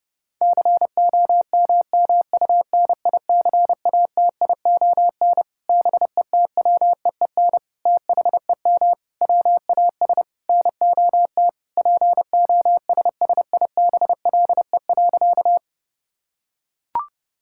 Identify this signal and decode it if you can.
Morse code, heard as a keyed tone, about 30 wpm, 700 Hz. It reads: COMMUNICATION BETWEEN THEM WAS NOT POSSIBLE.
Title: Sentences from Top 1000 Words - Review Part 6 - 30wpm